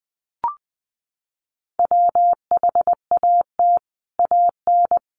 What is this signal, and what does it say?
Morse code, heard as a keyed tone, about 20 wpm, 700 Hz. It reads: WHAT AN
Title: Sentences from Top 1300 Words 20wpm